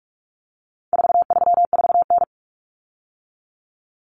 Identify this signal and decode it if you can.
Morse code, heard as a keyed tone, about 44 wpm, 700 Hz. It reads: 434N
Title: Common Cut Numbers - 4-Digit 45wpm